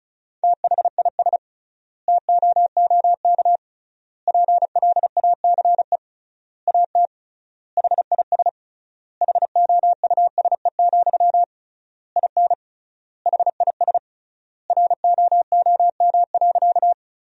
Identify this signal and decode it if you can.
Morse code, heard as a keyed tone, about 34 wpm, 700 Hz. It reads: THIS TOOK PLACE AT HIS HOUSE, IN HIS ROOM.
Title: 3x Word-Spacing - Sentences from Top 300 Words - 35wpm